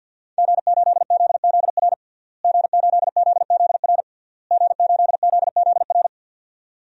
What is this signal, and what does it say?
Morse code, heard as a keyed tone, about 53 wpm, 700 Hz. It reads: G8ZZR G8ZZR G8ZZR
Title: DXCC-100 Call Signs - 5 Characters - Repeated 3x - 50wpm